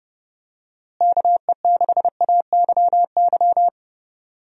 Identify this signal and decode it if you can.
Morse code, heard as a keyed tone, about 30 wpm, 700 Hz. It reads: KE6AYY
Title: USA Call Signs - 6 Characters - Repeated 2x - 30wpm